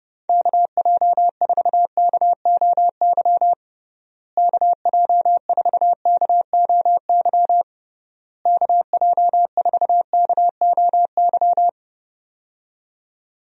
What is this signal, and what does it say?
Morse code, heard as a keyed tone, about 32 wpm, 700 Hz. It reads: KJ4KOY KJ4KOY KJ4KOY
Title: USA Call Signs - 6 Characters - Repeated 3x - 30wpm